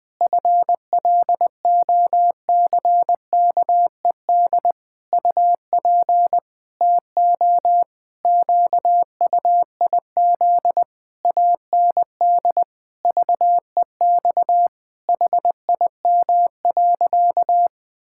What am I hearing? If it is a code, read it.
Morse code, heard as a keyed tone, about 20 wpm, 700 Hz. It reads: FLOCKED UP TO QUIZ AND VEX HIM.
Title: Pangram Challenge v2 - 20wpm